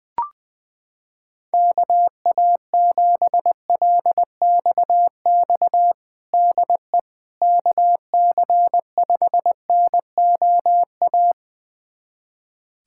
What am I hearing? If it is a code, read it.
Morse code, heard as a keyed tone, about 20 wpm, 700 Hz. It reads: KA7LXX DE KC5NOA
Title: USA Callsign DE Callsign 20wpm